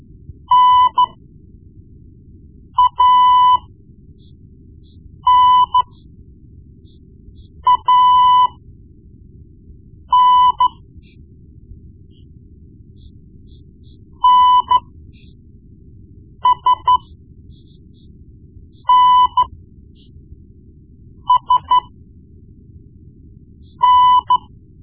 Morse code, heard as a keyed tone, about 14 wpm, spaced as for 4 wpm, 1000 Hz. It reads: NANAN NSNSN
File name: Morse Lessons / Morse Lessons / Tape-1B